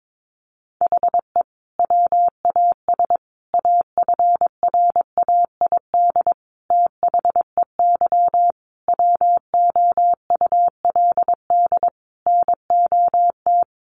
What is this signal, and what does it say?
Morse code, heard as a keyed tone, about 22 wpm, 700 Hz. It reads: HE WAS AFRAID THEY WOULD NOT
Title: Sentences from Top 1500 Words 22wpm